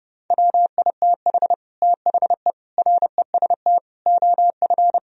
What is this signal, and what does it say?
Morse code, heard as a keyed tone, about 31 wpm, 700 Hz. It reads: WITH THE REST OF